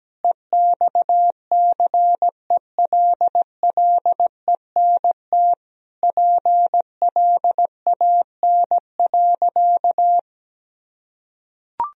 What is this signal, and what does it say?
Morse code, heard as a keyed tone, about 17 wpm, 700 Hz. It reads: EXCELLENT PLAN.
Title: Sentences from Top 1000 Words - Review Part 2 - 17wpm